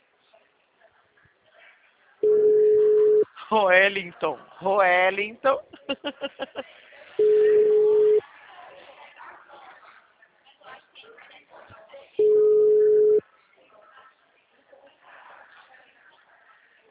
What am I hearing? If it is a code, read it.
Ringback tone (Europe, 425 Hz), with speech.